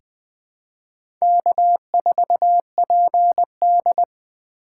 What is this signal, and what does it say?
Morse code, heard as a keyed tone, about 20 wpm, 700 Hz. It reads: K4PD
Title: USA Call Signs - 4 Characters - 8x Character Spacing - 20wpm